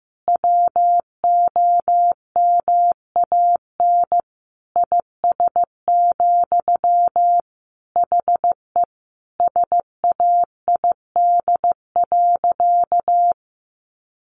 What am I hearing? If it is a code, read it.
Morse code, heard as a keyed tone, about 15 wpm, 700 Hz. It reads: WOMAN IS, HE SAID.
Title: Sentences from Top 1200 Words 15wpm